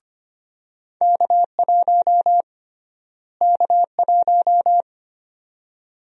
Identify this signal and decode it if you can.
Morse code, heard as a keyed tone, about 25 wpm, 700 Hz. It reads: K1 K1
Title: USA Call Sign Prefixes - Repeated 2x - 25wpm